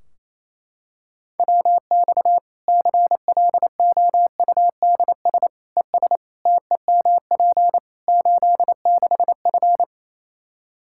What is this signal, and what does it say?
Morse code, heard as a keyed tone, about 28 wpm, 700 Hz. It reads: WX CLOUDS ES TEMP 86F
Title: Practice QSO - Exchange 2 - Info - 28wpm